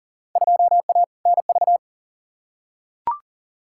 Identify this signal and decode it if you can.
Morse code, heard as a keyed tone, about 40 wpm, 700 Hz. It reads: 2A NV
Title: Field Day Exchanges - 40wpm